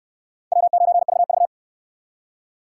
Morse code, heard as a keyed tone, about 69 wpm, 700 Hz. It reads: W9VV